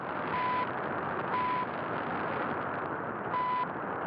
There is no code.